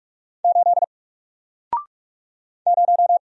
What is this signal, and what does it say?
Morse code, heard as a keyed tone, about 44 wpm, 700 Hz. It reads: 8 0
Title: Single Character (Letter:Number:Punctuation:Prosign) 45wpm